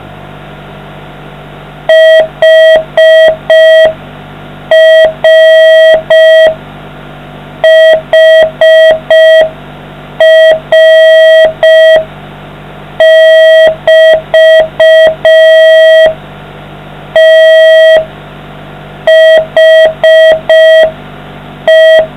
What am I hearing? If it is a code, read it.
Morse code, heard as a keyed tone, about 5 wpm, spaced as for 4 wpm, 640 Hz. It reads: HRHR<BT>THE